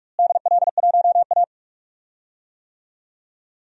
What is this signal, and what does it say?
Morse code, heard as a keyed tone, about 41 wpm, 700 Hz. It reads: DL1A
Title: DXCC-100 Call Signs - 4 Characters - Repeated 3x - 45wpm